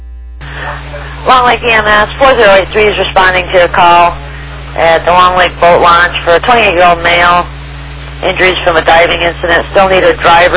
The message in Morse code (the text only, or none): none